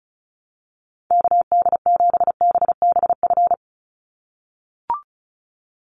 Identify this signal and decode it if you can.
Morse code, heard as a keyed tone, about 34 wpm, 700 Hz. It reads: KD7BBF
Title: USA Call Signs 35wpm